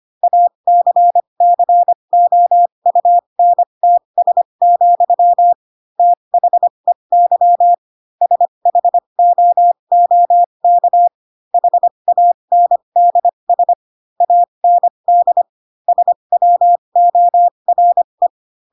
Morse code, heard as a keyed tone, about 25 wpm, spaced as for 22 wpm, 700 Hz. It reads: ACCOUNTS, THEY SHOOK HANDS AND SWORE
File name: Pinocchio / Pinocchio ch2 18 25 ik1hge